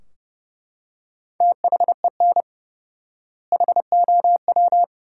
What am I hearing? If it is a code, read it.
Morse code, heard as a keyed tone, about 30 wpm, 700 Hz. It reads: THEN HOW
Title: 4x Word-Spacing - Sentences from Top 100 Words - 30wpm